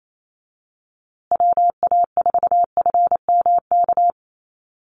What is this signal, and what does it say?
Morse code, heard as a keyed tone, about 28 wpm, 700 Hz. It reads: WA4FMK
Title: USA Call Signs Repeated 3x 28wpm